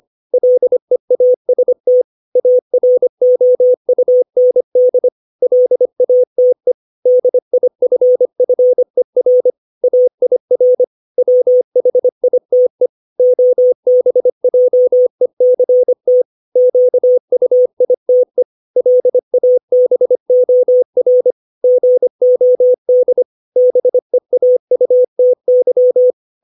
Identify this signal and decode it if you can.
Morse code, heard as a keyed tone, about 25 wpm, 500 Hz. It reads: LEAST AROUND LATE DIFFER AIR WHITE OBJECT QUITE LABOR GOD BEAUTY
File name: 632 words / 632 Words 25WPM